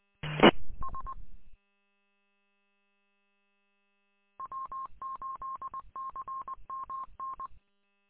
Morse code, heard as a keyed tone, about 20 wpm, 1070 Hz. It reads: E S W8CMN
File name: w8cmn- -combined